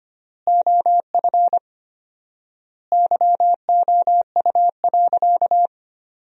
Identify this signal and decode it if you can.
Morse code, heard as a keyed tone, about 25 wpm, 700 Hz. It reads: OF YOU.
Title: 4x Word-Spacing - Sentences from Top 500 Words - 25wpm